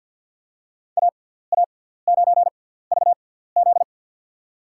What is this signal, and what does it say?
Morse code, heard as a keyed tone, about 52 wpm, spaced as for 17 wpm, 700 Hz. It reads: AA9VZ
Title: USA Call Signs - 6x Character Spacing - 50wpm